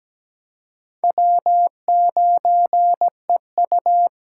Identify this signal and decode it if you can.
Morse code, heard as a keyed tone, about 20 wpm, 700 Hz. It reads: W9EU
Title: Speed Racing - USA 1x2 Call Signs 17wpm